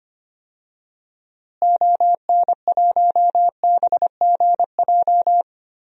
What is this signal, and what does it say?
Morse code, heard as a keyed tone, about 25 wpm, 700 Hz. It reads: ON1BGJ
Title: DXCC-100 Call Signs - 6 Characters - Repeated 2x - 25wpm